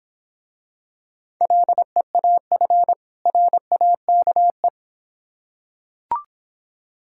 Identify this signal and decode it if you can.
Morse code, heard as a keyed tone, about 26 wpm, 700 Hz. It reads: LEAF RAKE